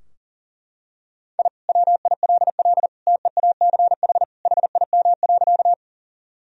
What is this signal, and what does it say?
Morse code, heard as a keyed tone, about 40 wpm, 700 Hz. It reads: I WILL TEACH HIM.
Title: Sentences from Top 1000 Words - Review Part 2 - 40wpm